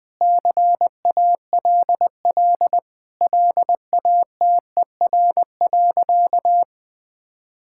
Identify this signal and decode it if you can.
Morse code, heard as a keyed tone, about 20 wpm, 700 Hz. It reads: CALL LATER.
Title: Sentences with 4-Word Phrases - 20wpm